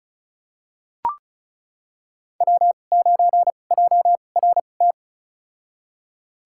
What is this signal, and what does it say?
Morse code, heard as a keyed tone, about 36 wpm, spaced as for 25 wpm, 690 Hz. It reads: W9JRT
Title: USA Call Signs - 5 Characters - 2x Character Spacing - 35wpm